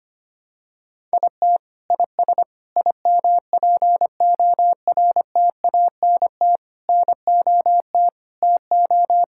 Morse code, heard as a keyed tone, about 25 wpm, 700 Hz. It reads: IT IS IMPORTANT NOT TO
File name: practice / 25wpm / Sentences with 9-Word Phrases - 25wpm